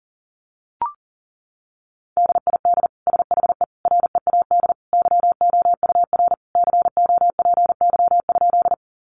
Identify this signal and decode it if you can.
Morse code, heard as a keyed tone, about 40 wpm, 700 Hz. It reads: DID SHE READ YOUR COPY?
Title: Sentences from Top 1400 Words 40wpm